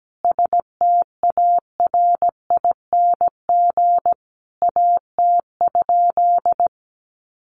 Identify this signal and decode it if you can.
Morse code, heard as a keyed tone, about 17 wpm, 700 Hz. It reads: STARING AT?
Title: A Sound Of Thunder 17wpm